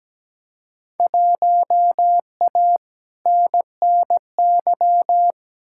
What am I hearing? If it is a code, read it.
Morse code, heard as a keyed tone, about 17 wpm, 700 Hz. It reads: 1A NNY
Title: Field Day Exchanges Encore - 17wpm